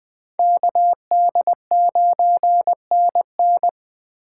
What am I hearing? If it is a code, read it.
Morse code, heard as a keyed tone, about 24 wpm, 700 Hz. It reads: KD9NN
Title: Speed Racing - USA Call Signs 20wpm